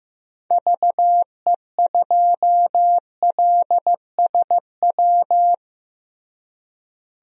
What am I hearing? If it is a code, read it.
Morse code, heard as a keyed tone, about 15 wpm, 700 Hz. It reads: VE2LSW
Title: DXCC-100 Call Signs - 6 Characters - Repeated 2x - 15wpm